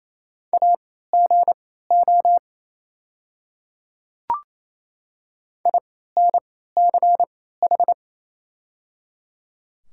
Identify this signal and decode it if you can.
Morse code, heard as a keyed tone, about 27 wpm, spaced as for 16 wpm, 700 Hz. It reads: AGO INCH
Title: 3x Character Spacing - Top 500 Words - 28wpm